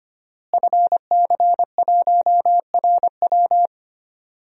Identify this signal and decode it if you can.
Morse code, heard as a keyed tone, about 25 wpm, 700 Hz. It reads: FC1RW
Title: DXCC-100 Call Signs - 5 Characters - Repeated 3x - 25wpm